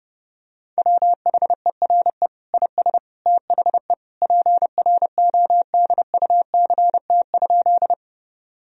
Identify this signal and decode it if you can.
Morse code, heard as a keyed tone, about 30 wpm, 700 Hz. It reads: WHERE IS THE PRODUCT?
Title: Sentences from Top 1000 Words - Review Part 2 - 30wpm